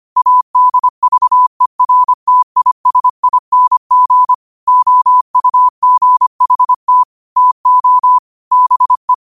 Morse code, heard as a keyed tone, about 25 wpm, 1000 Hz. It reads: ADVERTISING OUGHT TO BE
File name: qotdfiles / 25wpm